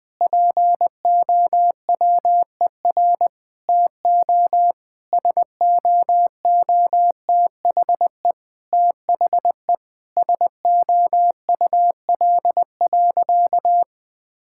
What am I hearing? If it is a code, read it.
Morse code, heard as a keyed tone, about 20 wpm, 700 Hz. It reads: POWER TO SOOTHE THE SOUL.
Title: Difficult Sentences with Hints - 20wpm